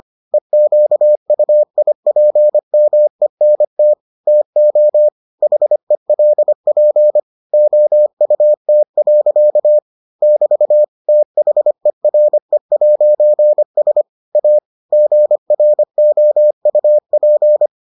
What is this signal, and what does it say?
Morse code, heard as a keyed tone, about 25 wpm, 600 Hz. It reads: EQUIPMENT TO HELP OUT. <BT> THERE'S A GROUP